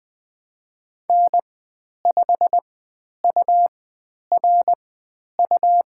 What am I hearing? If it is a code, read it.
Morse code, heard as a keyed tone, about 20 wpm, spaced as for 10 wpm, 700 Hz. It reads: N5URU
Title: 102 Lesson 17g - Letter u - Call Signs 20wpm